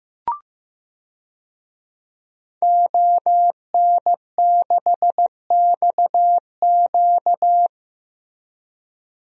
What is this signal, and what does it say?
Morse code, heard as a keyed tone, about 15 wpm, 700 Hz. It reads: ON6XQ
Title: DXCC-100 Call Signs - 5 Characters - Repeated 2x - 15wpm